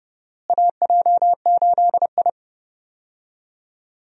Morse code, heard as a keyed tone, about 30 wpm, 700 Hz. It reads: AJ8I
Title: USA Call Signs - 4 Characters - Repeated 2x - 30wpm